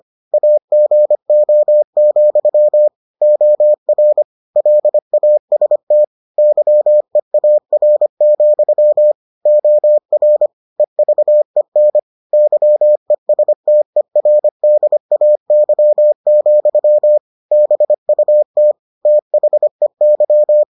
Morse code, heard as a keyed tone, about 25 wpm, 600 Hz. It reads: AGO, OR LAST YEAR, OR EVEN YESTERDAY, BUT THEY